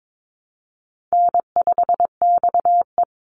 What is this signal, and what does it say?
Morse code, heard as a keyed tone, about 22 wpm, 700 Hz. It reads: N5XE
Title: Call Signs - 22wpm